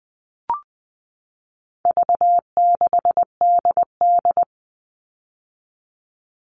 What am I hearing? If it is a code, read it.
Morse code, heard as a keyed tone, about 20 wpm, 700 Hz. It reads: V6DD
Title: All Cut Numbers - 4-Digit 20wpm